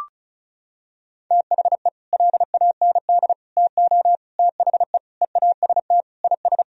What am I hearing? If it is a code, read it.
Morse code, heard as a keyed tone, about 34 wpm, 700 Hz. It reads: THE LAND TO THE EAST IS
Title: Sentences from Top 1000 Words - Review Part 4 - 35wpm